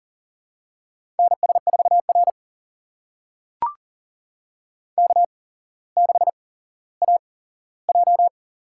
Morse code, heard as a keyed tone, about 42 wpm, spaced as for 17 wpm, 700 Hz. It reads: NS4R K6AJ
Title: USA Call Signs - 4 Characters - 8x Character Spacing - 40wpm